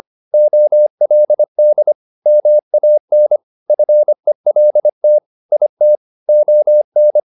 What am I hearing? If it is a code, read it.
Morse code, heard as a keyed tone, about 25 wpm, 600 Hz. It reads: OLD MAN FELT IT ON
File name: the old man and the sea part 1 25wpm